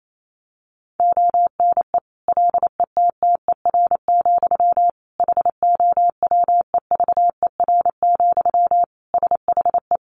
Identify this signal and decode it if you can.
Morse code, heard as a keyed tone, about 28 wpm, 700 Hz. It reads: ONE LETTER, HOWEVER, SHE
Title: Sentences from Top 1200 Words 28wpm